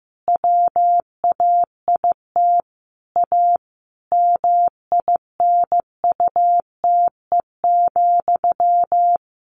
Morse code, heard as a keyed tone, about 15 wpm, 700 Hz. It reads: WAIT A MINUTE,
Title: Sentences from Top 1500 Words 15wpm